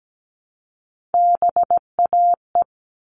Morse code, heard as a keyed tone, about 17 wpm, 700 Hz. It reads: BAE
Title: All Cut Numbers - 3-Digit 17wpm